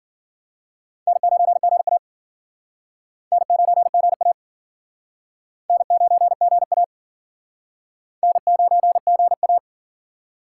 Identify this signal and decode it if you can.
Morse code, heard as a keyed tone, about 51 wpm, 700 Hz. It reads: N9GA N9GA N9GA N9GA